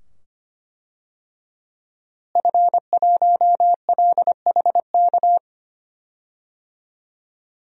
Morse code, heard as a keyed tone, about 25 wpm, 700 Hz. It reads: F1LHK
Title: DXCC-100 Call Signs - 5 Characters - Repeated 2x - 25wpm